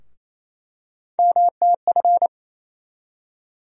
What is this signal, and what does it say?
Morse code, heard as a keyed tone, about 27 wpm, 700 Hz. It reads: MTF
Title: USA Call Sign Postfixes - 3 Letters 28wpm